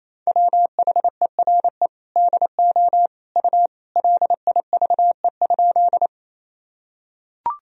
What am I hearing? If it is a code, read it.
Morse code, heard as a keyed tone, about 28 wpm, 700 Hz. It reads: WHERE DO U LIVE?
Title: Sentences from Top 200 Words and Abbreviations 28wpm